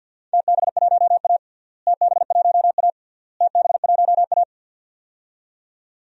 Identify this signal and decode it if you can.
Morse code, heard as a keyed tone, about 51 wpm, 700 Hz. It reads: TB1A TB1A TB1A
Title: DXCC-100 Call Signs - 4 Characters - Repeated 3x - 50wpm